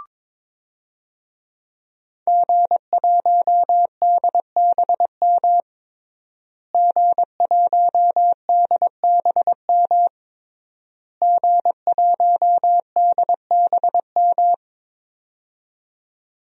Morse code, heard as a keyed tone, about 22 wpm, 700 Hz. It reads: G1DBM G1DBM G1DBM
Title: DXCC-100 Call Signs - 5 Characters - Repeated 3x - 22wpm